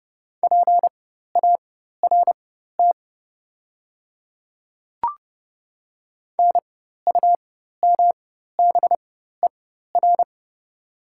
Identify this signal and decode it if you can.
Morse code, heard as a keyed tone, about 30 wpm, spaced as for 14 wpm, 700 Hz. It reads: PART NUMBER
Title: 4x Character Spacing - Top 200 Words - 30wpm